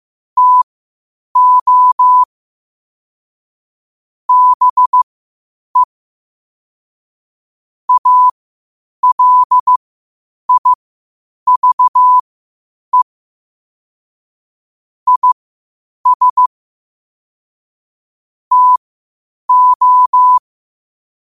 Morse code, heard as a keyed tone, about 15 wpm, spaced as for 8 wpm, 1000 Hz. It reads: TO BE ALIVE IS TO
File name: qotdfiles / 8wpm